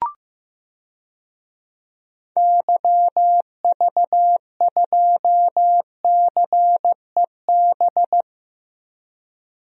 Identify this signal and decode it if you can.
Morse code, heard as a keyed tone, about 15 wpm, 700 Hz. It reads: YV2CEB
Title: DXCC-100 Call Signs - 6 Characters - 15wpm